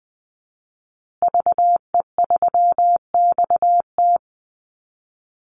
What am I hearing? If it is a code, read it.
Morse code, heard as a keyed tone, about 20 wpm, 700 Hz. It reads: VE3XT
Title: Call Signs Repeated 2x - 20wpm